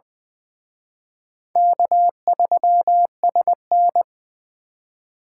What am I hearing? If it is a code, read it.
Morse code, heard as a keyed tone, about 25 wpm, 700 Hz. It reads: K3SN
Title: Speed Racing - USA Call Signs 20wpm